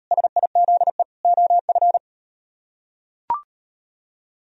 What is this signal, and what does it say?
Morse code, heard as a keyed tone, about 36 wpm, 700 Hz. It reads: SIZE OF